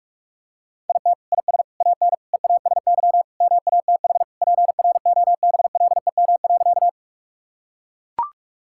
Morse code, heard as a keyed tone, about 41 wpm, 700 Hz. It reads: IT IS AN EASY MATH PROBLEM.
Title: Sentences from Top 1000 Words - Review Part 6 - 45wpm